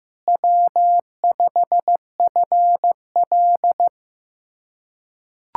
Morse code, heard as a keyed tone, about 15 wpm, 700 Hz. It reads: W5FL
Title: USA Callsign DE Callsign - 4 Characters - 15wpm